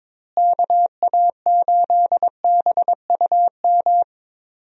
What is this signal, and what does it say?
Morse code, heard as a keyed tone, about 22 wpm, 700 Hz. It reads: KA8BUM
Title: USA Call Signs - 6 Characters - Repeated 3x - 22wpm